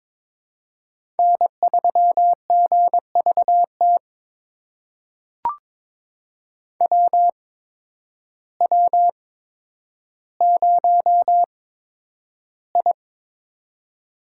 Morse code, heard as a keyed tone, about 22 wpm, spaced as for 10 wpm, 700 Hz. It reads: N3GVT WW0I
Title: USA Call Signs - 8x Character Spacing - 22wpm